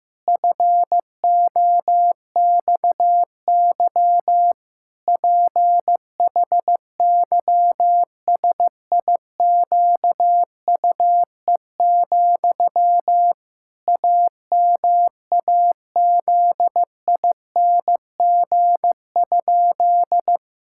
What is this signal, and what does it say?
Morse code, heard as a keyed tone, about 15 wpm, 700 Hz. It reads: FOXY PHYSIQUE, AMAZING?
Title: Pangram Challenge v2 - 15wpm